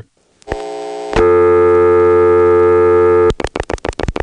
Dial tone (North America), 350 + 440 Hz.
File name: phonetrips / PrinceWmCoVA5HQ